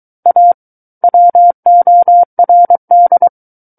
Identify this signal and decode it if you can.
Morse code, heard as a keyed tone, about 23 wpm, 700 Hz. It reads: A WORD
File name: Book One / 14-Chapter 14 - 23 WPM